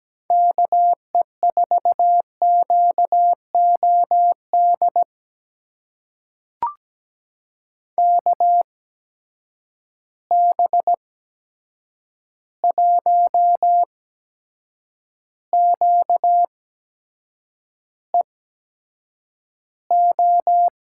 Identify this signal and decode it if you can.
Morse code, heard as a keyed tone, about 17 wpm, spaced as for 7 wpm, 700 Hz. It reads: KE4QOD KB1QEO
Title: USA Call Signs - 6 Characters - 8x Character Spacing - 17wpm